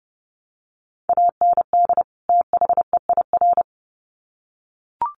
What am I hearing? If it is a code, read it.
Morse code, heard as a keyed tone, about 30 wpm, 700 Hz. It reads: AND THEIR
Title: Sets of 2 Words 30wpm